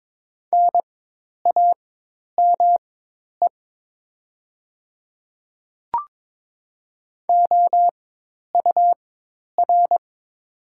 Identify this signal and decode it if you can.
Morse code, heard as a keyed tone, about 22 wpm, spaced as for 10 wpm, 700 Hz. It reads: NAME OUR
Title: 4x Character Spacing - Top 500 Words - 22wpm